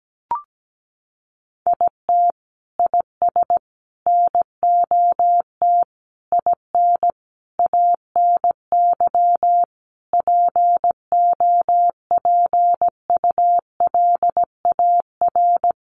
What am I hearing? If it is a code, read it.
Morse code, heard as a keyed tone, about 17 wpm, 700 Hz. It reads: IT IS NOT IN ANY POPULAR
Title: Sentences from Top 1200 Words 17wpm